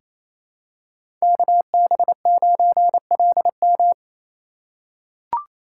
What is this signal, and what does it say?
Morse code, heard as a keyed tone, about 28 wpm, 700 Hz. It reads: KB9LM
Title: USA Call Signs - 2x Character Spacing - 28wpm